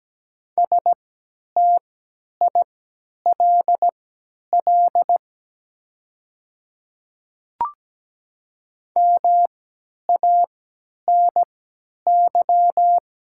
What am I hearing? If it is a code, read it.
Morse code, heard as a keyed tone, about 17 wpm, spaced as for 10 wpm, 700 Hz. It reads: STILL MANY
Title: 3x Character Spacing - Top 300 Words - 17wpm